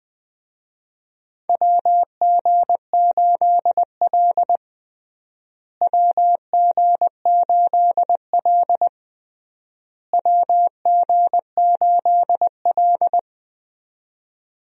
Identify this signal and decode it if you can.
Morse code, heard as a keyed tone, about 20 wpm, 700 Hz. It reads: WG8L WG8L WG8L
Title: USA Call Signs - 4 Characters - Repeated 3x - 20wpm